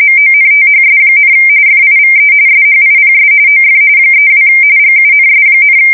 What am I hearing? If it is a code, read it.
RTTY, about 45.45 baud. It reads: MONTHS, FIVE SHIPS OF 32 GUNS,
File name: fsk170 / usnavy1